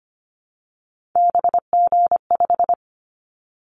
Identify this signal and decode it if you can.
Morse code, heard as a keyed tone, about 25 wpm, 700 Hz. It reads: BG5
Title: Callsign Prefixes for Top 100 DXCC - 25wpm